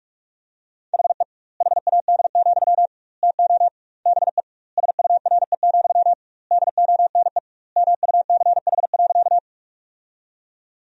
Morse code, heard as a keyed tone, about 41 wpm, 700 Hz. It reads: HE HAD, TO BE SURE, DONE MUCH.
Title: 2x Word-Spacing - Sentences from Top 500 Words - 45wpm